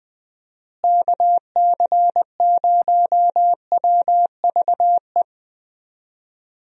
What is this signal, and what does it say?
Morse code, heard as a keyed tone, about 20 wpm, 700 Hz. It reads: KC0WVE
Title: USA Call Signs - 6 Characters - Repeated 2x - 20wpm